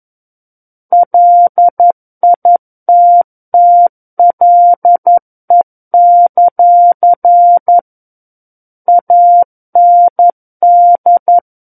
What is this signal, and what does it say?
Morse code, heard as a keyed tone, about 11 wpm, 700 Hz. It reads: LITTLE; AND
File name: Book One / 02-Chapter 2 - 11 WPM